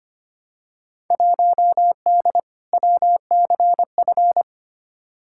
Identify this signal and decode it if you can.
Morse code, heard as a keyed tone, about 25 wpm, 700 Hz. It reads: 1D WCF
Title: Field Day Exchanges - 25wpm